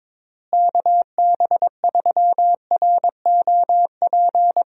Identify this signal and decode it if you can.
Morse code, heard as a keyed tone, about 22 wpm, 700 Hz. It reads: KB3ROP
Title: USA Call Signs - 6 Characters - Repeated 3x - 22wpm